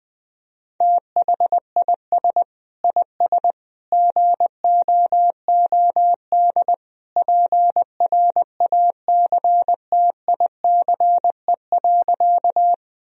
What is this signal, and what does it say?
Morse code, heard as a keyed tone, about 20 wpm, 700 Hz. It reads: THIS IS GOOD PRACTICE.
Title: Sentences from Top 1000 Words - Review Part 4 - 20wpm